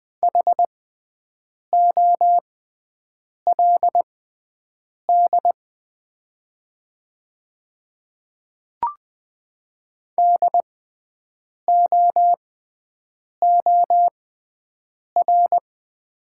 Morse code, heard as a keyed tone, about 20 wpm, spaced as for 7 wpm, 700 Hz. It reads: HOLD DOOR
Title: 6x Character Spacing - Top 500 Words - 20wpm